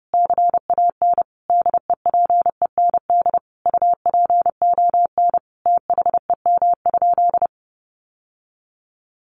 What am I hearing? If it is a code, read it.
Morse code, heard as a keyed tone, about 30 wpm, 700 Hz. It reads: CAN DEPEND UPON THEM?
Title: Sentences from Top 1300 Words 30wpm